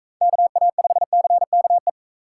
Morse code, heard as a keyed tone, about 44 wpm, 700 Hz. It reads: KA5CKE